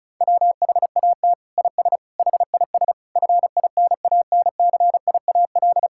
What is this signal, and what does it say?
Morse code, heard as a keyed tone, about 36 wpm, 700 Hz. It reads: WHAT IS HIS FINANCIAL
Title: Sentences from Top 1000 Words - Review Part 5 - 35wpm